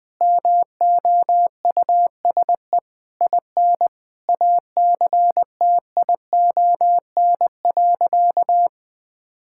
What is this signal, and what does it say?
Morse code, heard as a keyed tone, about 20 wpm, 700 Hz. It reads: MOUSE IN ACTION.
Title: Sentences from Top 1000 Words - Review Part 5 - 20wpm